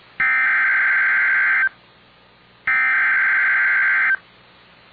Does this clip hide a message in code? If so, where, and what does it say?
Yes: Emergency Alert System (SAME) tones (header bursts) from 0.2 to 4.1 s, heard as harsh buzzing bursts.